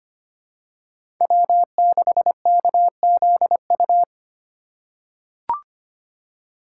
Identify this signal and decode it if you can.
Morse code, heard as a keyed tone, about 25 wpm, 700 Hz. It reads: W6KZU
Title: USA Call Signs - 5 Characters - 2x Character Spacing - 25wpm